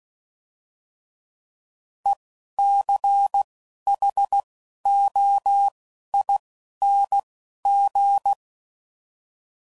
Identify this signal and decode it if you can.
Morse code, heard as a keyed tone, about 16 wpm, 800 Hz. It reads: E C H O I N G